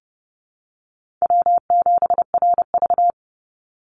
Morse code, heard as a keyed tone, about 30 wpm, 700 Hz. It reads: W7RV
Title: NA SOTA Chaser Callsigns 30wpm